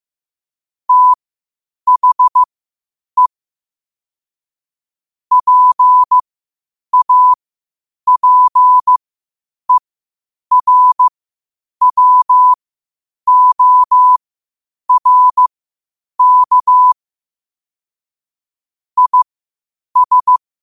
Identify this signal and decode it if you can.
Morse code, heard as a keyed tone, about 15 wpm, spaced as for 8 wpm, 1000 Hz. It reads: THE PAPERWORK IS
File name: qotdfiles / 8wpm